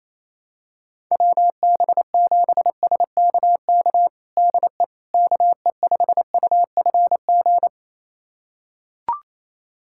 Morse code, heard as a keyed tone, about 27 wpm, 700 Hz. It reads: WB7SKK DE KE5UFG
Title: USA Callsign DE Callsign - 6 Characters - 28wpm